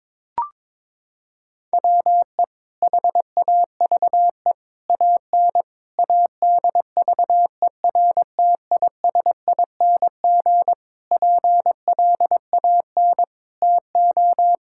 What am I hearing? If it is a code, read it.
Morse code, heard as a keyed tone, about 22 wpm, 700 Hz. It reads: WE HAVE AN ADVERTISING PLAN TO
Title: Sentences from Top 1000 Words - Review Part 6 - 22wpm